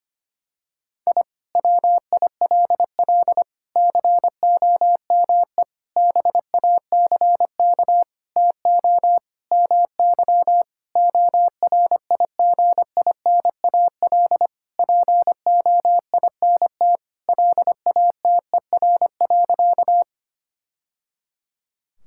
Morse code, heard as a keyed tone, about 25 wpm, 700 Hz. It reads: I WILL COME BACK TO MY ORIGINAL POINT LATER.
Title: Sentences from Top 1000 Words - Review Part 6 - 25wpm